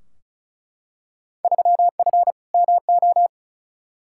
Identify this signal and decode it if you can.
Morse code, heard as a keyed tone, about 34 wpm, 700 Hz. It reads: 3F MO
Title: Field Day Exchanges Encore - 35wpm